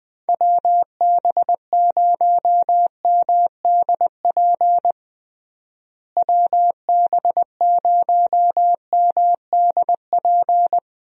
Morse code, heard as a keyed tone, about 21 wpm, 700 Hz. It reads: WB0MDP WB0MDP
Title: USA Call Signs - 6 Characters - Repeated 3x - 20wpm